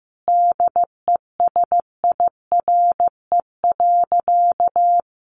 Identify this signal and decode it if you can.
Morse code, heard as a keyed tone, about 15 wpm, 700 Hz. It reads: DESIRE.
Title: Sentences from Top 1100 Words 15wpm